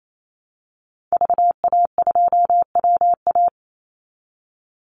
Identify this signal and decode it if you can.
Morse code, heard as a keyed tone, about 28 wpm, 700 Hz. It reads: VA2WA
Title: Call Signs Repeated 3x - 28wpm